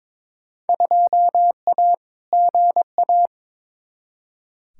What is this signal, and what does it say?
Morse code, heard as a keyed tone, about 22 wpm, 700 Hz. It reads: 2A GA
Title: Field Day Exchanges - 22wpm